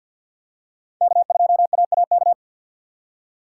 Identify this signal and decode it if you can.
Morse code, heard as a keyed tone, about 50 wpm, 700 Hz. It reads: K2AAK
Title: USA Call Signs - 5 Characters - 6x Character Spacing - 50wpm